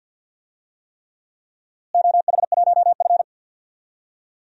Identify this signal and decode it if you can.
Morse code, heard as a keyed tone, about 51 wpm, 700 Hz. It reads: OH1F
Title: DXCC-100 Call Signs - 4 Characters - Repeated 3x - 50wpm